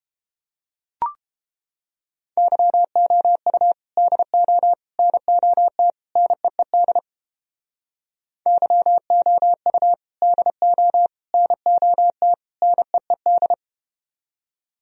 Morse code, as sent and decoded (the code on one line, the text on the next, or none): -.-- --- ..- / -.. --- / -. --- - / -. . -.. / -.-- --- ..- / -.. --- / -. --- - / -. . -..
YOU DO NOT NEED YOU DO NOT NEED